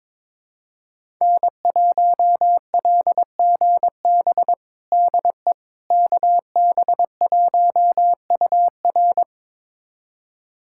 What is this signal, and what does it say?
Morse code, heard as a keyed tone, about 22 wpm, 700 Hz. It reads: N1LGB DE KB1UR
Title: USA Callsign DE Callsign - 5 Characters - 22wpm